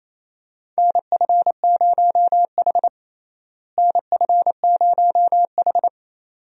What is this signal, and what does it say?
Morse code, heard as a keyed tone, about 27 wpm, 700 Hz. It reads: NF0H NF0H
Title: USA Call Signs - 4 Characters - Repeated 2x - 28wpm